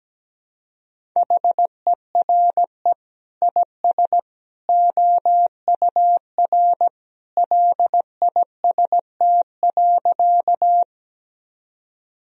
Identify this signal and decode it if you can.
Morse code, heard as a keyed tone, about 17 wpm, 700 Hz. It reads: HERE IS OUR LIST.
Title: Sentences from Top 1000 Words - Review Part 2 - 17wpm